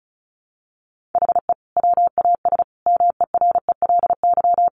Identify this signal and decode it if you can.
Morse code, heard as a keyed tone, about 34 wpm, 700 Hz. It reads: HE WAS MERELY
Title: Sentences from Top 1400 Words 35wpm